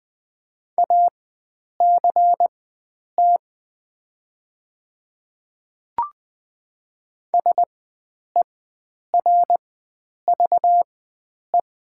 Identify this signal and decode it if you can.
Morse code, heard as a keyed tone, about 20 wpm, spaced as for 9 wpm, 700 Hz. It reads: ACT SERVE
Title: 4x Character Spacing - Top 500 Words - 20wpm